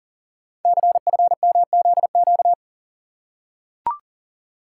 Morse code, heard as a keyed tone, about 42 wpm, 690 Hz. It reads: CFMZQ
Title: Code Groups - 5 Letters - 40wpm